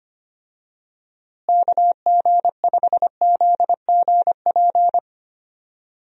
Morse code, heard as a keyed tone, about 25 wpm, 700 Hz. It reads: KG5ZGP
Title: USA Call Signs - 6 Characters - Repeated 3x - 25wpm